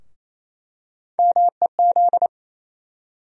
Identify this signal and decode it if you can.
Morse code, heard as a keyed tone, about 28 wpm, 700 Hz. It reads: MEZ
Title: USA Call Sign Postfixes - 3 Letters 28wpm